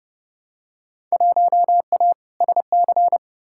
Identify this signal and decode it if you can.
Morse code, heard as a keyed tone, about 30 wpm, 700 Hz. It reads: 1A SC
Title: Field Day Exchanges - 30wpm